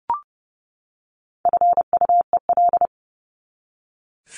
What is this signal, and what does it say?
Morse code, heard as a keyed tone, about 30 wpm, 700 Hz. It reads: FUEL
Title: Top 1751-2000 Words 30wpm